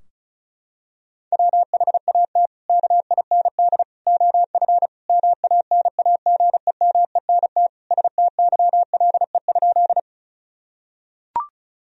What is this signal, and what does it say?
Morse code, heard as a keyed tone, about 36 wpm, 700 Hz. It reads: WHAT KIND OF MANAGEMENT STYLE?
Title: Sentences from Top 1000 Words - Review Part 4 - 35wpm